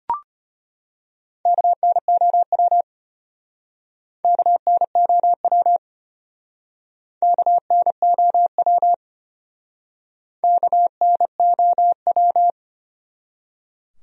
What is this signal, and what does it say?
Morse code, heard as a keyed tone, about 30 wpm, 700 Hz. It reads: KNOW KNOW KNOW KNOW
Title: Speed-Racing Top 500 Words 25wpm